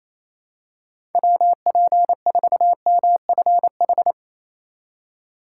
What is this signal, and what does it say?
Morse code, heard as a keyed tone, about 28 wpm, 700 Hz. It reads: WP4MFH
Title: USA Call Signs - 6 Characters - 4x Character Spacing - 28wpm